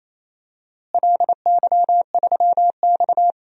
Morse code, heard as a keyed tone, about 28 wpm, 700 Hz. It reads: LY3X
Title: DXCC-100 Call Signs - 4 Characters - Repeated 3x - 28wpm